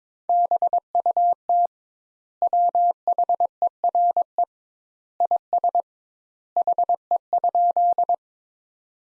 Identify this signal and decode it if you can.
Morse code, heard as a keyed tone, about 22 wpm, 700 Hz. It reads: BUT WHERE IS HE?
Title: Sentences Top 200 Words - 22wpm